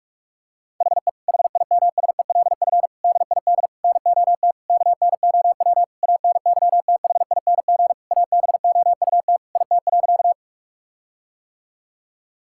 Morse code, heard as a keyed tone, about 46 wpm, 700 Hz. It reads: HE HIMSELF DID NOT KNOW ANYTHING ABOUT IT.
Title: Sentences from Top 1600 Words 45wpm